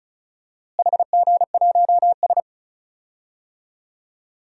Morse code, heard as a keyed tone, about 36 wpm, 700 Hz. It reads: HG1S
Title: DXCC-100 Call Signs - 4 Characters - Repeated 2x - 35wpm